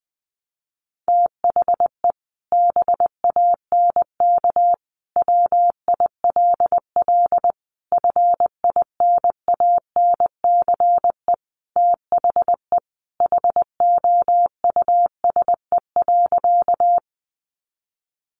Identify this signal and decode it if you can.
Morse code, heard as a keyed tone, about 20 wpm, 700 Hz. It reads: THE BANK WILL FINANCE THE HOUSE.
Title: Sentences from Top 1500 Words 20wpm